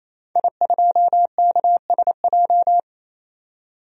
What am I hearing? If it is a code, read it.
Morse code, heard as a keyed tone, about 28 wpm, 700 Hz. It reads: I2KSJ
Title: DXCC-100 Call Signs - 5 Characters - 28wpm